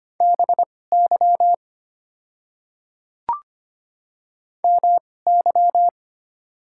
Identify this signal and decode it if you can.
Morse code, heard as a keyed tone, about 25 wpm, spaced as for 18 wpm, 690 Hz. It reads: BY MY
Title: 2x Character Spacing - Top 100 Words - 25wpm